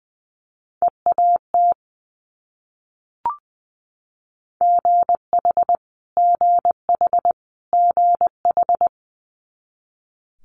Morse code, heard as a keyed tone, about 19 wpm, 700 Hz. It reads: EAT GH GH GH
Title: Multi-letter Phonemes Repeated 3x 20wpm